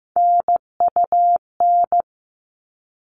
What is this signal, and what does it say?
Morse code, heard as a keyed tone, about 15 wpm, 700 Hz. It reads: NUN
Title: All Cut Numbers - 3-Digit 15wpm